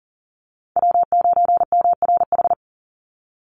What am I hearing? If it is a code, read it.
Morse code, heard as a keyed tone, about 40 wpm, 700 Hz. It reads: W9MRH
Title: NA SOTA Chaser Callsigns 40wpm